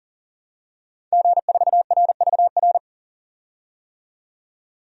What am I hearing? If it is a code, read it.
Morse code, heard as a keyed tone, about 40 wpm, 700 Hz. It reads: G4RVR
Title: DXCC-100 Call Signs - 5 Characters - 40wpm